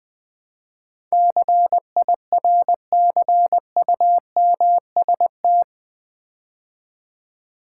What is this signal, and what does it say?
Morse code, heard as a keyed tone, about 20 wpm, 700 Hz. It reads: CIRCUMST
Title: Build A Word - 4 Letters At A Time - 12-Letter Words - 20wpm